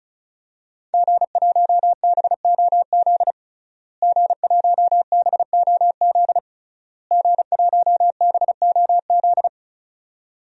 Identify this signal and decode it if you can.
Morse code, heard as a keyed tone, about 36 wpm, 700 Hz. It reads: G1BOZ G1BOZ G1BOZ
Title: DXCC-100 Call Signs - 5 Characters - Repeated 3x - 35wpm